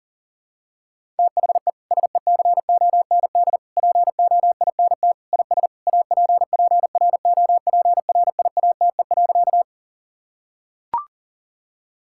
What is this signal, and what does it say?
Morse code, heard as a keyed tone, about 37 wpm, 700 Hz. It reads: THE SECOND POINT IS APPROPRIATE.
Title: Sentences from Top 1000 Words - Review Part 5 - 40wpm